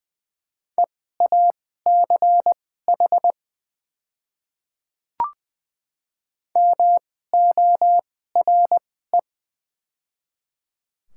Morse code, heard as a keyed tone, about 20 wpm, spaced as for 14 wpm, 700 Hz. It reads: EACH MORE
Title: 2x Character Spacing - Top 200 Words - 20wpm